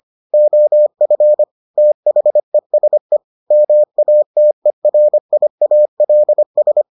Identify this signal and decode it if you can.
Morse code, heard as a keyed tone, about 25 wpm, 600 Hz. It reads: OF THESE MATERIALS